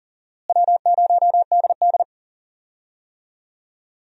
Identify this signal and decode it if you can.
Morse code, heard as a keyed tone, about 40 wpm, 700 Hz. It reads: W0DD
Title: USA Call Signs - 4 Characters - Repeated 2x - 40wpm